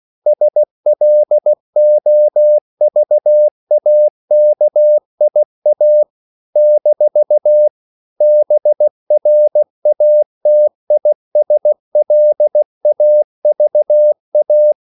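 Morse code, heard as a keyed tone, about 16 wpm, 600 Hz. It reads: SLOVAKIA - BRATISLAVA
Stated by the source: CW practice of national capitals at 16 words per minute, from ON6ZQ.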